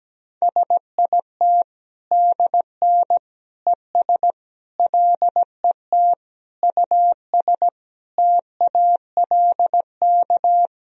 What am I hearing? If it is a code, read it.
Morse code, heard as a keyed tone, about 17 wpm, 700 Hz. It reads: SIT DN ES LET US TALK
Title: Sentences from Top 500 Words and Abbreviations 17wpm